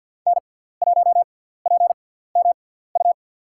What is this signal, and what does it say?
Morse code, heard as a keyed tone, about 47 wpm, spaced as for 17 wpm, 700 Hz. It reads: N1PMU
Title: USA Call Signs - 5 Characters - 6x Character Spacing - 50wpm